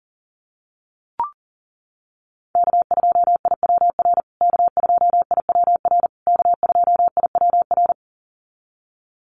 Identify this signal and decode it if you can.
Morse code, heard as a keyed tone, about 40 wpm, 700 Hz. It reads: K2IWR K2IWR K2IWR
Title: USA Call Signs Repeated 3x 40wpm